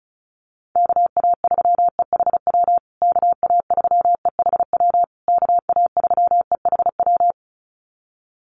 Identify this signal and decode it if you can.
Morse code, heard as a keyed tone, about 34 wpm, 700 Hz. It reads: KA3EHW KA3EHW KA3EHW
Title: USA Call Signs Repeated 3x 35wpm